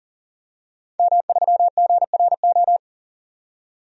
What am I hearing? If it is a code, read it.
Morse code, heard as a keyed tone, about 40 wpm, 700 Hz. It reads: M3GRO
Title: DXCC-100 Call Signs - 5 Characters - 40wpm